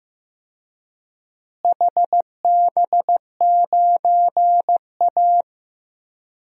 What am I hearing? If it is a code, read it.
Morse code, heard as a keyed tone, about 15 wpm, 700 Hz. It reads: HB9A
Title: DXCC-100 Call Signs - 4 Characters - Repeated 2x - 15wpm